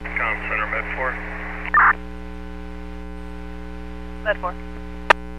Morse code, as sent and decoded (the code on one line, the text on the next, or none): none